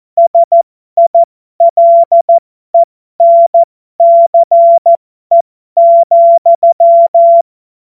Morse code, heard as a keyed tone, about 14 wpm, spaced as for 12 wpm, 680 Hz. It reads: SILENCE,